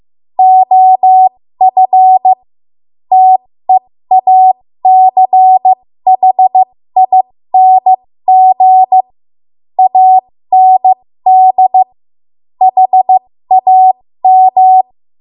Morse code, heard as a keyed tone, about 15 wpm, spaced as for 13 wpm, 750 Hz. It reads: OF TEACHING AND HAM
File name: Morse / 13WPM